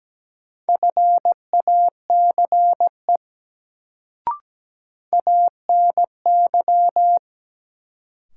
Morse code, heard as a keyed tone, about 17 wpm, 700 Hz. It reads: FACE ANY
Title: Sets of 3 Words - New Word Drill - 17wpm